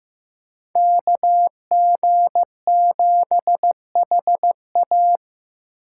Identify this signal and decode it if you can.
Morse code, heard as a keyed tone, about 15 wpm, 700 Hz. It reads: KG7HA
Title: USA Call Signs - 5 Characters - 6x Character Spacing - 15wpm